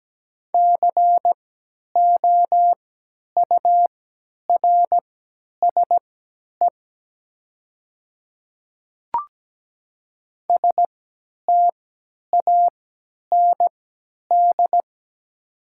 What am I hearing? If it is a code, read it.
Morse code, heard as a keyed tone, about 17 wpm, spaced as for 10 wpm, 700 Hz. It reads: COURSE STAND
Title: 3x Character Spacing - Top 500 Words - 17wpm